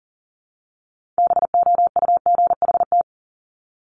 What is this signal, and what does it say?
Morse code, heard as a keyed tone, about 40 wpm, 700 Hz. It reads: BOUGHT
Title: Multi-letter Phonemes 40wpm